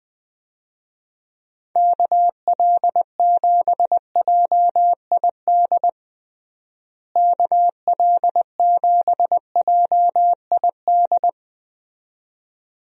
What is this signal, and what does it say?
Morse code, heard as a keyed tone, about 20 wpm, 700 Hz. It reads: KL7JID KL7JID
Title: DXCC-100 Call Signs - 6 Characters - Repeated 2x - 20wpm